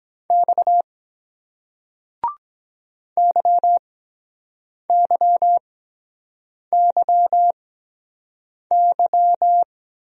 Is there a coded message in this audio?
Yes: Morse code from 0.3 to 9.6 s, heard as a keyed tone, about 23 wpm, 700 Hz. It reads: X Y Y Y Y